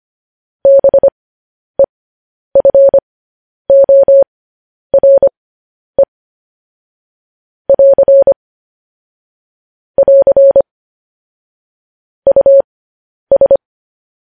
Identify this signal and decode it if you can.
Morse code, heard as a keyed tone, about 25 wpm, spaced as for 10 wpm, 550 Hz. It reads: BEFORE <AR> <AR> US
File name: podcast / News headlines-1x-25x10